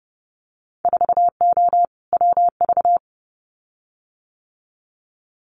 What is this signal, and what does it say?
Morse code, heard as a keyed tone, about 30 wpm, 700 Hz. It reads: 4O WV
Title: Winter Field Day Exchanges Encore 30wpm